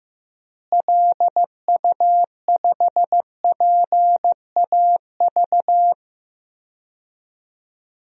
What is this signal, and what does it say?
Morse code, heard as a keyed tone, about 16 wpm, 700 Hz. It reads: LU5PAV
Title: DXCC-100 Call Signs - 6 Characters - Repeated 2x - 15wpm